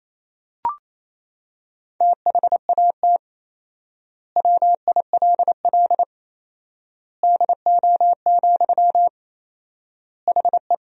Morse code, heard as a keyed tone, about 28 wpm, 700 Hz. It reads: THAT WILL DO, HE
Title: 4x Word-Spacing - Sentences from Top 100 Words - 28wpm